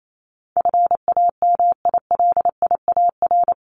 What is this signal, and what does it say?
Morse code, heard as a keyed tone, about 28 wpm, 700 Hz. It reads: FAMILIAR
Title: Top 751-1000 Words 28wpm